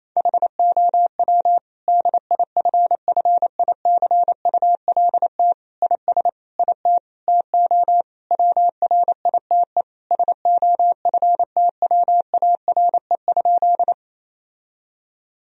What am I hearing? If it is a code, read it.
Morse code, heard as a keyed tone, about 28 wpm, 700 Hz. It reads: HOW DIFFICULT IS IT TO WRITE SOFTWARE?
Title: Sentences from Top 1000 Words - Review Part 3 - 28wpm